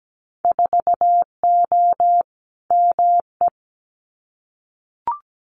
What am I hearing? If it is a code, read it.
Morse code, heard as a keyed tone, about 17 wpm, 700 Hz. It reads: 4O ME
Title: Winter Field Day Exchanges Encore 17wpm